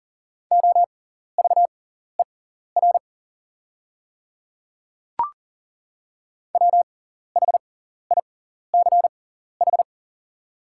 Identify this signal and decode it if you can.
Morse code, heard as a keyed tone, about 40 wpm, spaced as for 14 wpm, 700 Hz. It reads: OVER WHICH
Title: 6x Character Spacing - Top 100 Words - 40wpm